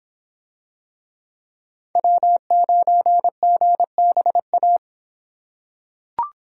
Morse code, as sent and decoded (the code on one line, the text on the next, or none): .-- ----. --. -... .-
W9GBA